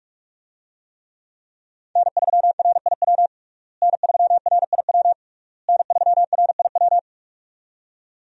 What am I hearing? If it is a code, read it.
Morse code, heard as a keyed tone, about 46 wpm, 700 Hz. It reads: N3RIW N3RIW N3RIW
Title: USA Call Signs - 5 Characters - Repeated 3x - 45wpm